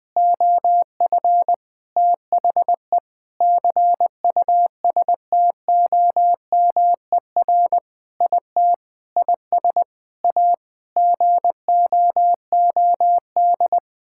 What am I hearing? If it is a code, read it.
Morse code, heard as a keyed tone, about 20 wpm, 700 Hz. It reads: OF THE CUSTOMER IT IS A GOOD
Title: Sentences with 7-Word Phrases - 20wpm